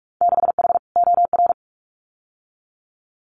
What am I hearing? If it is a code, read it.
Morse code, heard as a keyed tone, about 46 wpm, 700 Hz. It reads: 6H OR
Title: Winter Field Day Exchanges Encore 45wpm